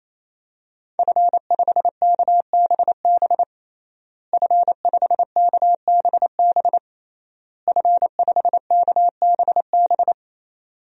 Morse code, heard as a keyed tone, about 28 wpm, 700 Hz. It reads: F5KBB F5KBB F5KBB
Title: DXCC-100 Call Signs - 5 Characters - Repeated 3x - 28wpm